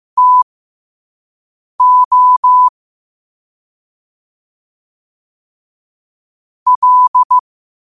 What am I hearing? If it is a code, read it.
Morse code, heard as a keyed tone, about 15 wpm, spaced as for 5 wpm, 1000 Hz. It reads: TO L
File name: qotdfiles / 5wpm